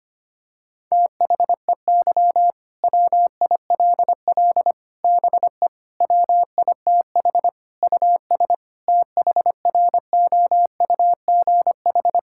Morse code, heard as a keyed tone, about 25 wpm, 700 Hz. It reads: THEY WILL BE WITH US THROUGH
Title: Sentences from Top 1600 Words 25wpm